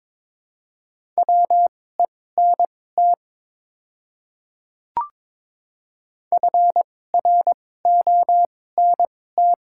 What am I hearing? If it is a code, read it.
Morse code, heard as a keyed tone, about 22 wpm, spaced as for 16 wpm, 700 Hz. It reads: WENT FRONT
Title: 2x Character Spacing - Top 500 Words - 22wpm